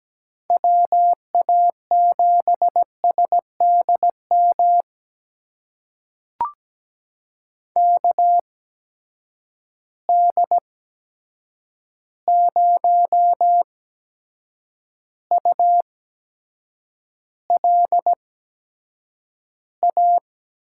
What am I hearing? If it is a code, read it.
Morse code, heard as a keyed tone, about 17 wpm, spaced as for 7 wpm, 700 Hz. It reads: WA7SDM KD0ULA
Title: USA Call Signs - 6 Characters - 8x Character Spacing - 17wpm